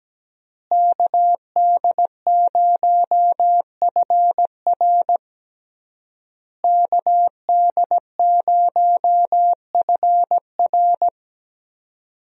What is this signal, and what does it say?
Morse code, heard as a keyed tone, about 17 wpm, 700 Hz. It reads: KD0FR KD0FR
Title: USA Call Signs - 5 Characters - Repeated 3x - 17wpm